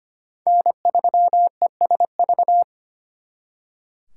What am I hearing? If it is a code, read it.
Morse code, heard as a keyed tone, about 31 wpm, 700 Hz. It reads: N3ESV
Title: Speed Racing - USA 1x3 Call Signs 25wpm